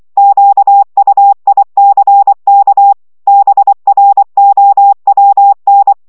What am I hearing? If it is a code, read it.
Morse code, heard as a keyed tone, about 24 wpm, 800 Hz. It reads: QUICK BROWN
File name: szamolo / CW 20 baud